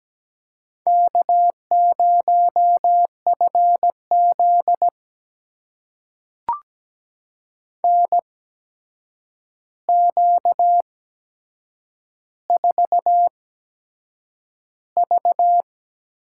Morse code, heard as a keyed tone, about 17 wpm, spaced as for 7 wpm, 700 Hz. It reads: K0FZ NQ4V
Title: USA Call Signs - 4 Characters - 8x Character Spacing - 17wpm